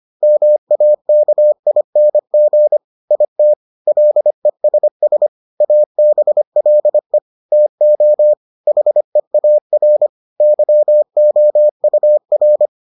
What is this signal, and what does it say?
Morse code, heard as a keyed tone, about 25 wpm, 600 Hz. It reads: MAKING IT LESS ABLE TO HEAR YOUR